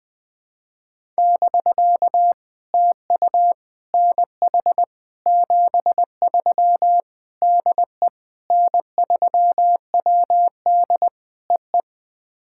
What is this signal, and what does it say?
Morse code, heard as a keyed tone, about 21 wpm, 700 Hz. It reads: <BK> TU NH 73 DE N3WD EE
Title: POTA QSO Practice - 20wpm